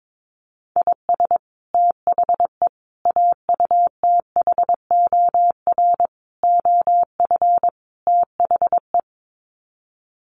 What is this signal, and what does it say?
Morse code, heard as a keyed tone, about 22 wpm, 700 Hz. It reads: IS THE AUTHOR OF THE
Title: Sets of 5 Words Encore 22wpm